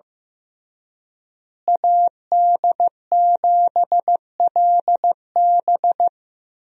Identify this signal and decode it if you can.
Morse code, heard as a keyed tone, about 15 wpm, 700 Hz. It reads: AD7LB
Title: USA Call Signs - 5 Characters - Repeated 3x - 15wpm